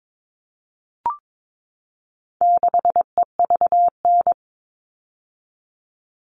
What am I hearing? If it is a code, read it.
Morse code, heard as a keyed tone, about 22 wpm, 700 Hz. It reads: 6EVN
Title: All Cut Numbers - 4-Digit 22wpm